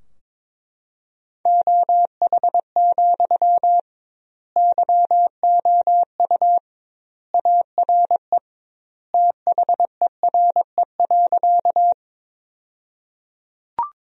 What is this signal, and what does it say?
Morse code, heard as a keyed tone, about 21 wpm, 700 Hz. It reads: OH, YOU ARE THERE.
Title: 2x Word-Spacing - Sentences from Top 500 Words - 22wpm